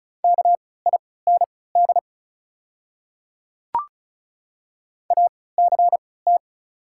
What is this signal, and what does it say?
Morse code, heard as a keyed tone, about 34 wpm, spaced as for 20 wpm, 700 Hz. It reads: KIND ACT
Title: 3x Character Spacing - Top 300 Words - 35wpm